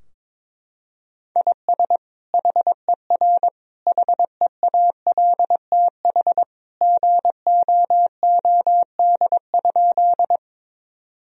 Morse code, heard as a keyed tone, about 22 wpm, 700 Hz. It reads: IS HER HEALTH GOOD?
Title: Sentences from Top 1000 Words - Review Part 3 - 22wpm